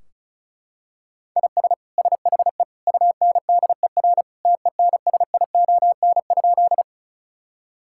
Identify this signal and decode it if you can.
Morse code, heard as a keyed tone, about 36 wpm, 700 Hz. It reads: IS SHE UNDER TENSION?
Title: Sentences from Top 1600 Words 35wpm